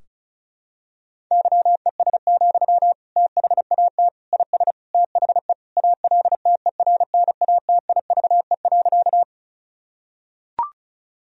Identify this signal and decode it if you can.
Morse code, heard as a keyed tone, about 34 wpm, 700 Hz. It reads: YES, THAT IS THE ALTERNATIVE.
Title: Sentences from Top 1000 Words - Review Part 4 - 35wpm